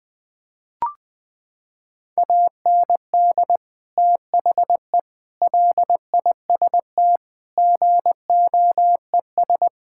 Morse code, heard as a keyed tone, about 20 wpm, 700 Hz. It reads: AND THE LIST GOES
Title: Sentences with 7-Word Phrases - 20wpm